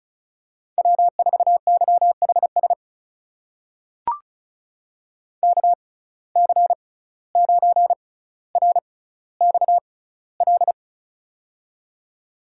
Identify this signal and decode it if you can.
Morse code, heard as a keyed tone, about 36 wpm, spaced as for 17 wpm, 700 Hz. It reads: W4YHS KC9RXL
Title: USA Call Signs - 6x Character Spacing - 35wpm